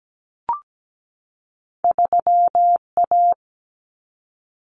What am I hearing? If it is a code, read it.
Morse code, heard as a keyed tone, about 17 wpm, 700 Hz. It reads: 3A
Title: Common Cut Numbers - 2-Digit 17wpm